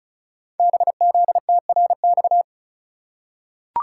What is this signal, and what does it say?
Morse code, heard as a keyed tone, about 36 wpm, 700 Hz. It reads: BZTRX
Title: Code Groups - 5 Letter:Numbers - 35wpm